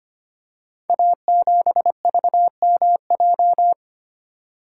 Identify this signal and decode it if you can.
Morse code, heard as a keyed tone, about 25 wpm, 700 Hz. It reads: A7VMJ
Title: Code Groups - 5 Letter:Numbers - 25wpm